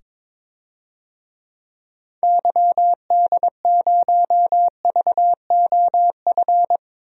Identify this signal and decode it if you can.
Morse code, heard as a keyed tone, about 22 wpm, 700 Hz. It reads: YD0VOF
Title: DXCC-100 Call Signs - 6 Characters - 22wpm